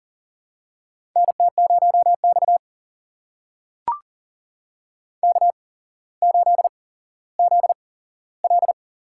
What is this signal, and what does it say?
Morse code, heard as a keyed tone, about 44 wpm, spaced as for 17 wpm, 700 Hz. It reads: NT0X K8ZL
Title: USA Call Signs - 4 Characters - 8x Character Spacing - 40wpm